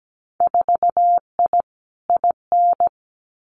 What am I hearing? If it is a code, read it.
Morse code, heard as a keyed tone, about 17 wpm, 700 Hz. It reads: 4I IN
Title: Winter Field Day Exchanges Encore 17wpm